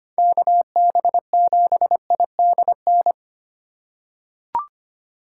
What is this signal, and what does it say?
Morse code, heard as a keyed tone, about 25 wpm, 700 Hz. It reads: KB7IDN
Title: USA Call Signs - 6 Characters - 2x Character Spacing - 25wpm